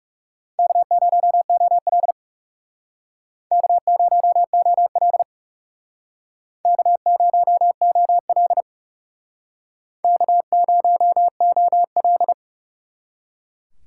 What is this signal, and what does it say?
Morse code, heard as a keyed tone, about 36 wpm, 700 Hz. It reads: K0OL K0OL K0OL K0OL
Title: Speed Racing - USA 1x2 Call Signs 30wpm